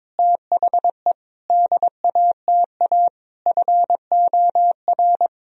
Morse code, heard as a keyed tone, about 22 wpm, 700 Hz. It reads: THE DATA FOR
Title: Sentences from Top 1000 Words - Review Part 3 - 22wpm